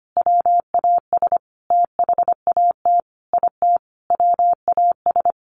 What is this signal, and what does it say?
Morse code, heard as a keyed tone, about 25 wpm, 700 Hz. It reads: WAS THAT IT WAS
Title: Sentences from Top 1400 Words 25wpm